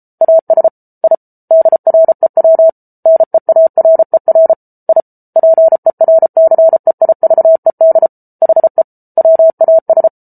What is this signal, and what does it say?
Morse code, heard as a keyed tone, about 34 wpm, 650 Hz. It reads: AS I DREW NEARER I PERCEIVED HE WAS